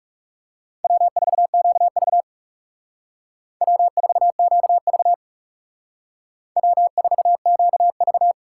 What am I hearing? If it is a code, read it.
Morse code, heard as a keyed tone, about 39 wpm, 700 Hz. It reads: W4QV W4QV W4QV